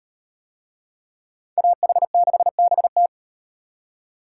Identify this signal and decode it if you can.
Morse code, heard as a keyed tone, about 37 wpm, 700 Hz. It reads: AH6BT